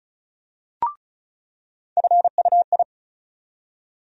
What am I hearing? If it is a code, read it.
Morse code, heard as a keyed tone, about 34 wpm, 700 Hz. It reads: FUI
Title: USA Call Sign Postfixes - 3 Letters 35wpm